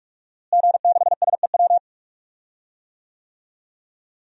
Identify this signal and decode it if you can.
Morse code, heard as a keyed tone, about 46 wpm, 700 Hz. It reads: G6SEW
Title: DXCC-100 Call Signs - 5 Characters - Repeated 2x - 45wpm